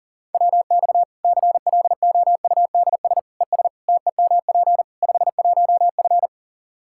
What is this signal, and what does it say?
Morse code, heard as a keyed tone, about 40 wpm, 700 Hz. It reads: WX CLOUDS ES TEMP 51F
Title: Practice QSO - Exchange 2 - Info - 40wpm